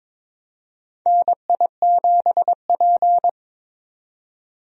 Morse code, heard as a keyed tone, about 22 wpm, 700 Hz. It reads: NI7P
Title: USA Call Signs - 4 Characters - 22wpm